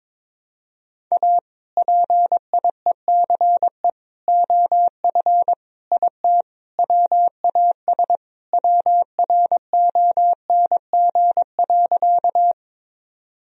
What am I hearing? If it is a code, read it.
Morse code, heard as a keyed tone, about 22 wpm, 700 Hz. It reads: A PIECE OF IT WAS WRONG.
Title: Sentences from Top 1000 Words - Review Part 2 - 22wpm